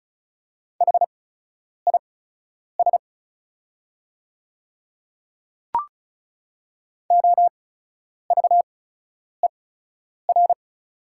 Morse code, heard as a keyed tone, about 36 wpm, spaced as for 10 wpm, 700 Hz. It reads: HIS OVER
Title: 8x Character Spacing - Top 200 Words - 35wpm